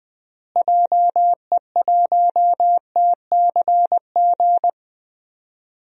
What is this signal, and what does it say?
Morse code, heard as a keyed tone, about 20 wpm, 700 Hz. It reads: JE1TCG
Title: DXCC-100 Call Signs - 6 Characters - Repeated 3x - 20wpm